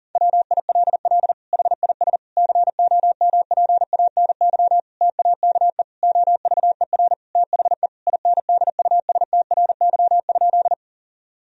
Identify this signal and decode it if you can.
Morse code, heard as a keyed tone, about 40 wpm, 700 Hz. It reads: WILL HIS COMPANY TAKE OVER THE INDUSTRY?
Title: Sentences from Top 1000 Words - Review Part 4 - 40wpm